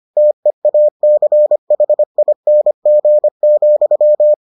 Morse code, heard as a keyed tone, about 25 wpm, 600 Hz. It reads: TEACHING,